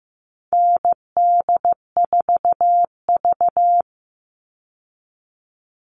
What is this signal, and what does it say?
Morse code, heard as a keyed tone, about 15 wpm, 700 Hz. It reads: ND4V
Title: Call Signs Repeated 2x - 15wpm